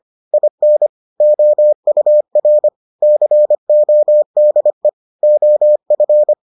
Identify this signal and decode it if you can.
Morse code, heard as a keyed tone, about 25 wpm, 600 Hz. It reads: IN OUR CODE OF